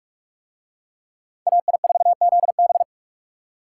Morse code, heard as a keyed tone, about 44 wpm, 700 Hz. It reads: AI4ZB